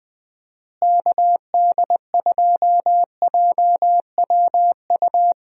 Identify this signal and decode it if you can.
Morse code, heard as a keyed tone, about 20 wpm, 700 Hz. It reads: KD2JWU
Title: USA Call Signs - 6 Characters - Repeated 3x - 20wpm